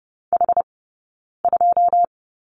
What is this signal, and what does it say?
Morse code, heard as a keyed tone, about 30 wpm, 700 Hz. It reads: H 2
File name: 092 Lesson 16d - Letter c - Two Characters 30wpm